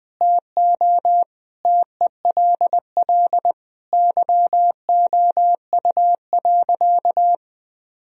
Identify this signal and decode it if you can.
Morse code, heard as a keyed tone, about 19 wpm, 700 Hz. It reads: TO TELL YOU.
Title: Sentences from Top 1000 Words - Review Part 3 - 20wpm